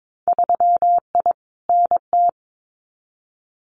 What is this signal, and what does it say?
Morse code, heard as a keyed tone, about 22 wpm, 700 Hz. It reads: 3I NT
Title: Winter Field Day Exchanges Encore 22wpm